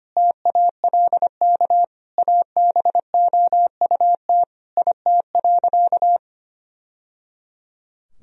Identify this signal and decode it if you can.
Morse code, heard as a keyed tone, about 25 wpm, 700 Hz. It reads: TALK ABOUT IT.
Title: Sentences from Top 1000 Words - Review Part 4 - 25wpm